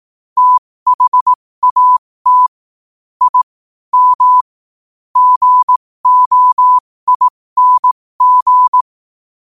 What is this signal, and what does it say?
Morse code, heard as a keyed tone, about 18 wpm, spaced as for 15 wpm, 1000 Hz. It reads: THAT I M GOING
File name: qotdfiles / 15wpm